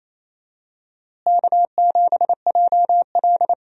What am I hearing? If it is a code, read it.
Morse code, heard as a keyed tone, about 28 wpm, 700 Hz. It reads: K7JL
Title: USA Call Signs - 4 Characters - Repeated 2x - 28wpm